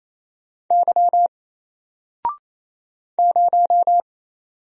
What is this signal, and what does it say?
Morse code, heard as a keyed tone, about 28 wpm, 700 Hz. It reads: Y 0
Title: Single Letter-Number Alt 28wpm